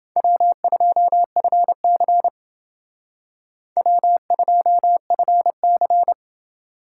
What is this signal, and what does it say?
Morse code, heard as a keyed tone, about 28 wpm, 700 Hz. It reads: W2FC W2FC